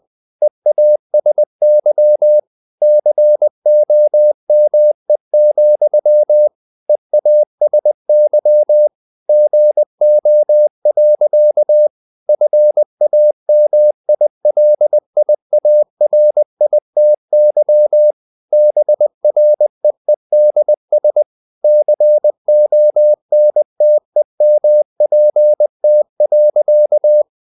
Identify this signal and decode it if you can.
Morse code, heard as a keyed tone, about 20 wpm, 600 Hz. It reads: EASY COME, EASY GO. FAMILIARITY BREEDS CONTEMPT.
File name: CW / Practice / EnglishSayings20